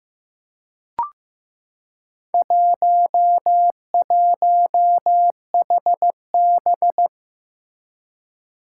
Morse code, heard as a keyed tone, about 15 wpm, 700 Hz. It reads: 11HB
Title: Four Challenging Letter-Number Combinations 15wpm